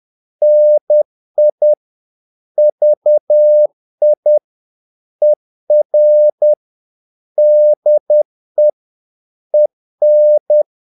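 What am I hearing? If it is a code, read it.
Morse code, heard as a keyed tone, about 10 wpm, 600 Hz. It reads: NI VI ER DE EN
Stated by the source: Morse practice of short Danish words at 10 wpm, from CW med Gnister.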